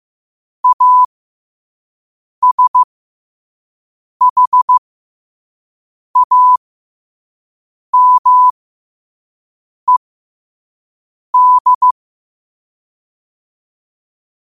Morse code, heard as a keyed tone, about 15 wpm, spaced as for 5 wpm, 1000 Hz. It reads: ASHAMED
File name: qotdfiles / 5wpm